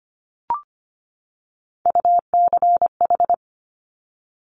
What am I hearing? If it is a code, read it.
Morse code, heard as a keyed tone, about 25 wpm, 700 Hz. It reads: UCH
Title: Top Three Letter Combinations in Words 25wpm